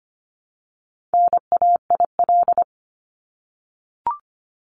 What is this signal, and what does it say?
Morse code, heard as a keyed tone, about 25 wpm, 700 Hz. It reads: NAIL
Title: Top 1501-1750 Words 25wpm